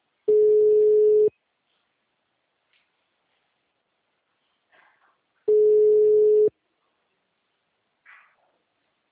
Ringback tone (Europe, 425 Hz).